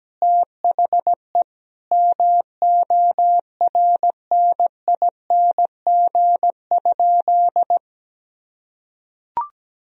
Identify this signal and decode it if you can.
Morse code, heard as a keyed tone, about 17 wpm, 700 Hz. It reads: THE MORNING?
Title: Sentences from Top 1000 Words - Review Part 2 - 17wpm